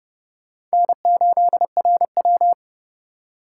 Morse code, heard as a keyed tone, about 37 wpm, 700 Hz. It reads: N8RW
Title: Speed Racing - USA Call Signs 30wpm